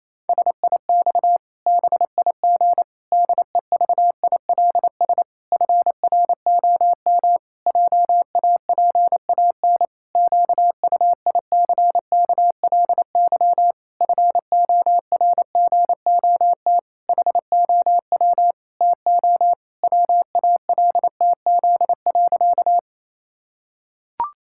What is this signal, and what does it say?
Morse code, heard as a keyed tone, about 28 wpm, 700 Hz. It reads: SIX BIG DEVILS FROM JAPAN QUICKLY FORGOT HOW TO WALTZ.
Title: Pangram Challenge v2 - 28wpm